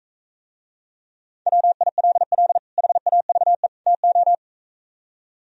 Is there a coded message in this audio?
Yes: Morse code from 1.5 to 4.3 s, heard as a keyed tone, about 45 wpm, 700 Hz. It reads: WILL HAVE TO